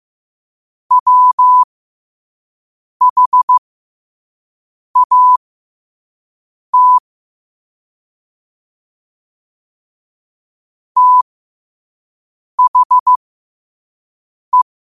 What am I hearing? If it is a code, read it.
Morse code, heard as a keyed tone, about 15 wpm, spaced as for 5 wpm, 1000 Hz. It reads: WHAT THE
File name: 5wpm